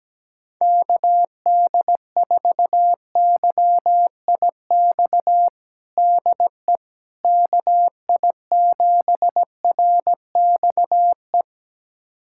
Morse code, heard as a keyed tone, about 17 wpm, 700 Hz. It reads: KD4YIX DE KI7RXE
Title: USA Callsign DE Callsign - 6 Characters - 17wpm